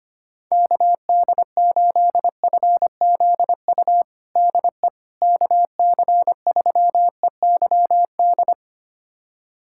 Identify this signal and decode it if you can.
Morse code, heard as a keyed tone, about 25 wpm, 700 Hz. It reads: KD8FZU DE KC3EYD
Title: USA Callsign DE Callsign 25wpm